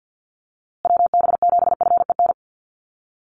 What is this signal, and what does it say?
Morse code, heard as a keyed tone, about 50 wpm, 700 Hz. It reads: WB7FEN
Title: USA Call Signs Repeated 3x 50wpm